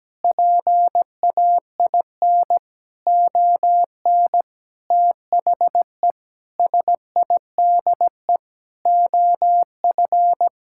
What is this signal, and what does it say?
Morse code, heard as a keyed tone, about 17 wpm, 700 Hz. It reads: PAIN ON THE SIDE OF
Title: Sentences from Top 1600 Words 17wpm